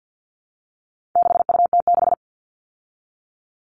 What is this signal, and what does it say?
Morse code, heard as a keyed tone, about 48 wpm, 700 Hz. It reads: 6UT6
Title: All Cut Numbers - 4-Digit 50wpm